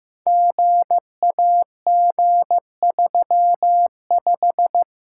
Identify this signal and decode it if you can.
Morse code, heard as a keyed tone, about 15 wpm, 700 Hz. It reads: GAG35
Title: Code Groups - 5 Letter:Numbers - 15wpm